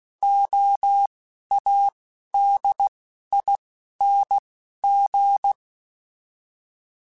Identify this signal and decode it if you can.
Morse code, heard as a keyed tone, about 16 wpm, spaced as for 12 wpm, 800 Hz. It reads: OADING
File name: Alphabetical / Alphabetical mp3s / Ls @10